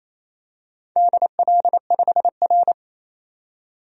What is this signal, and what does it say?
Morse code, heard as a keyed tone, about 28 wpm, 700 Hz. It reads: DL5R
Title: DXCC-100 Call Signs - 4 Characters - 28wpm